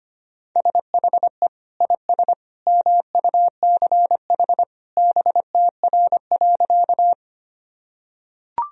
Morse code, heard as a keyed tone, about 25 wpm, 700 Hz. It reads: SHE IS MUCH BTR.
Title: Sentences from Top 400 Words and Abbreviations 25wpm